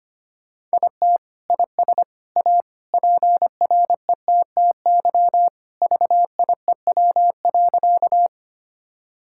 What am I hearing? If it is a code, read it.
Morse code, heard as a keyed tone, about 25 wpm, 700 Hz. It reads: IT IS A PRETTY VIEW.
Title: Sentences from Top 1000 Words - Review Part 6 - 25wpm